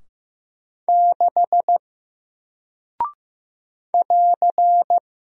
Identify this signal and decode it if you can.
Morse code, heard as a keyed tone, about 15 wpm, 700 Hz. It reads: 6 <AR>
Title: Single Character (Letter:Number:Punctuation:Prosign) Alt 15wpm